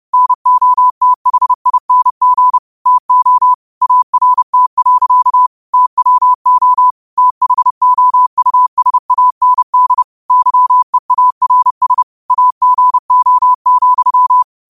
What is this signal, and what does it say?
Morse code, heard as a keyed tone, about 30 wpm, 1000 Hz. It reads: NOTHING TO ART. TWO THOUSAND YEARS AGO,
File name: qotdfiles / 30wpm